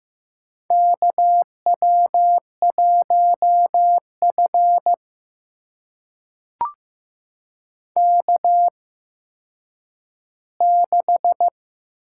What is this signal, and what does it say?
Morse code, heard as a keyed tone, about 15 wpm, spaced as for 9 wpm, 700 Hz. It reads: KW1F K6
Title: USA Call Signs - 4 Characters - 8x Character Spacing - 15wpm